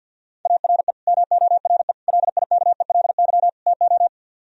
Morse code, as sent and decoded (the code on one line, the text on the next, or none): .- .-. / -- --- .-. / .-.. .. -.- . .-.. -.-- / - ---
ARE MORE LIKELY TO